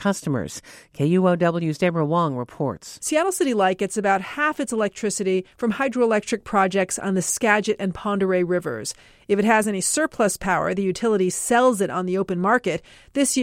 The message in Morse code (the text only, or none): none